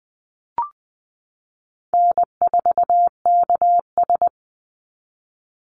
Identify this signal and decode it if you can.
Morse code, heard as a keyed tone, about 20 wpm, 700 Hz. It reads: N4KS
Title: Easy Call Signs Repeated 3x - 20wpm